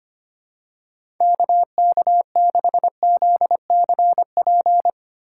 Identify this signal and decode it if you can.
Morse code, heard as a keyed tone, about 25 wpm, 700 Hz. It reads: KK6ZCP
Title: USA Call Signs - 6 Characters - 2x Character Spacing - 25wpm